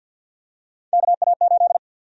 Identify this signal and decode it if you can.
Morse code, heard as a keyed tone, about 47 wpm, 700 Hz. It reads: KA8
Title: USA Call Sign Prefixes - 50wpm